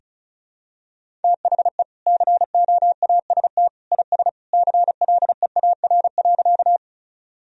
Morse code, heard as a keyed tone, about 34 wpm, 700 Hz. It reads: THE COAST IS CLEAR.
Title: Sentences from Top 1000 Words - Review Part 4 - 35wpm